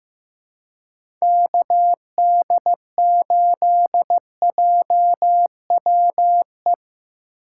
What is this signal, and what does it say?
Morse code, heard as a keyed tone, about 15 wpm, 700 Hz. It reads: KD8JWE
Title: USA Call Signs - 6 Characters - 2x Character Spacing - 15wpm